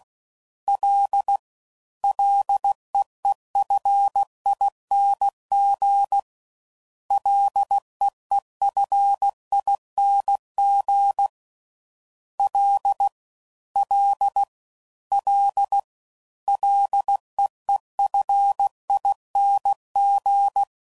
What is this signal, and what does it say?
Morse code, heard as a keyed tone, about 16 wpm, 800 Hz. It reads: L LEEFING LEEFING L L L LEEFING
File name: Prefixes / FLs & THs / Single Fs Ls